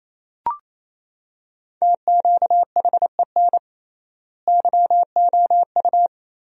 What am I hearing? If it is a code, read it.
Morse code, heard as a keyed tone, about 27 wpm, 700 Hz. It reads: TQHEN YOU
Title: 3x Word-Spacing - Sentences from Top 100 Words - 28wpm